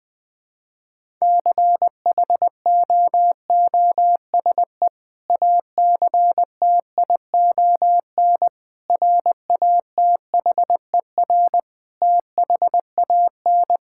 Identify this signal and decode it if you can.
Morse code, heard as a keyed tone, about 21 wpm, 700 Hz. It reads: CHOOSE ACTION RATHER THAN
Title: Sentences with 2-Word Phrases - 20wpm